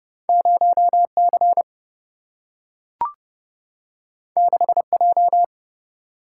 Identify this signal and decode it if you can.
Morse code, heard as a keyed tone, about 30 wpm, 700 Hz. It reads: 0C 6J
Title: Single-Letter Single-Number 30wpm